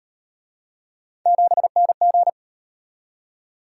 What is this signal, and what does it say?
Morse code, heard as a keyed tone, about 36 wpm, 700 Hz. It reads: 7NG